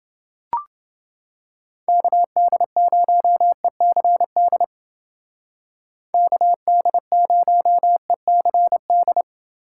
Morse code, heard as a keyed tone, about 28 wpm, 700 Hz. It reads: KD0ECD KD0ECD